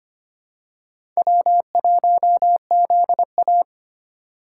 Morse code, heard as a keyed tone, about 25 wpm, 690 Hz. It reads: W1ZA
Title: USA Call Signs - 4 Characters - 4x Character Spacing - 25wpm